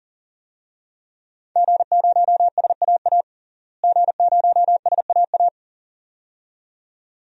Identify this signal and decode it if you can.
Morse code, heard as a keyed tone, about 40 wpm, 700 Hz. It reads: G0SAA G0SAA
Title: DXCC-100 Call Signs - 5 Characters - Repeated 2x - 40wpm